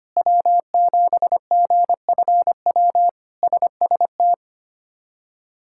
Morse code, heard as a keyed tone, about 25 wpm, 700 Hz. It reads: W7GFW SST
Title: K1USN SST QSO Practice - 25wpm